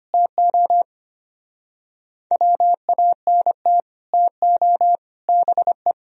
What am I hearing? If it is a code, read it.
Morse code, heard as a keyed tone, about 30 wpm, 700 Hz. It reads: TO WANT TO BE
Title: Speed-Racing - Sets of 5 Words 25wpm